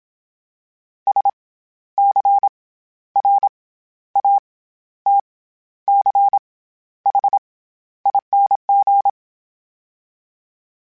Morse code, heard as a keyed tone, about 27 wpm, 800 Hz. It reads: S C R A T C H ING
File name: Special INGs-mp3s / Special INGs-mp3s / SlowToFaster ing @25